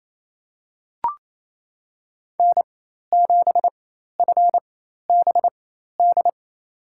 Morse code, heard as a keyed tone, about 28 wpm, spaced as for 13 wpm, 700 Hz. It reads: N7FBD
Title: USA Call Signs - 5 Characters - 4x Character Spacing - 28wpm